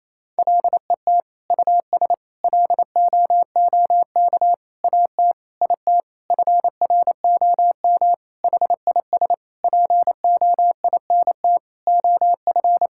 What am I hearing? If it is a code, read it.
Morse code, heard as a keyed tone, about 28 wpm, 700 Hz. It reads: LET US LOOK AT IT FROM HIS POINT OF
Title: Sentences from Top 1000 Words - Review Part 6 - 28wpm